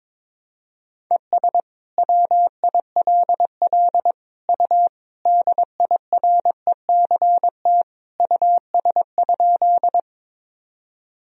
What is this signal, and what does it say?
Morse code, heard as a keyed tone, about 22 wpm, 700 Hz. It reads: ES WILL U DIRECT US?
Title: Sentences from Top 400 Words and Abbreviations 22wpm